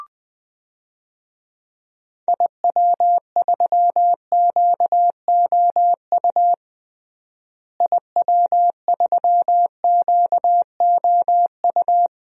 Morse code, heard as a keyed tone, about 20 wpm, 700 Hz. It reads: IW3QOU IW3QOU
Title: DXCC-100 Call Signs - 6 Characters - Repeated 3x - 20wpm